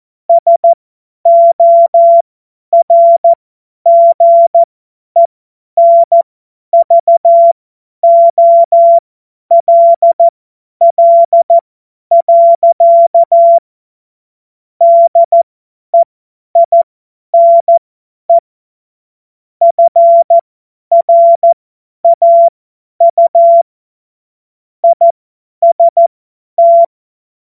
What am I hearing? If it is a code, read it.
Morse code, heard as a keyed tone, about 14 wpm, spaced as for 10 wpm, 680 Hz. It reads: SORGENVOLL. DEINE FRAU IST
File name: Morsekurs / Witze / Witz 04. April 2023 50 F70